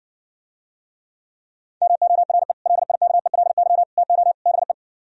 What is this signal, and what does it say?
Morse code, heard as a keyed tone, about 59 wpm, 700 Hz. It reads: MORE LIKELY TO BE